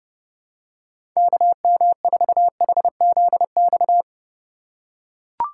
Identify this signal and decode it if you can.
Morse code, heard as a keyed tone, about 30 wpm, 700 Hz. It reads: KM4HZX
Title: USA Call Signs - 6 Characters - 3x Character Spacing - 30wpm